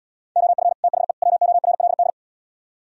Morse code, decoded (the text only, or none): OF THE WORLD